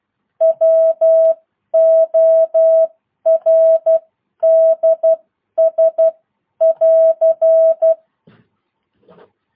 Morse code, heard as a keyed tone, about 12 wpm, spaced as for 10 wpm, 650 Hz. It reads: WORDS <AR>